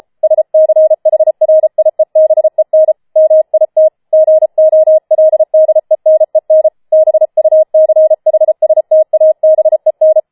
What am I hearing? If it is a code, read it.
Morse code, heard as a keyed tone, about 33 wpm, 630 Hz. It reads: SCHRIEBEN MIT GOLDENEN BUCHSTABEN
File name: Schneewittchen Teil5 34wpm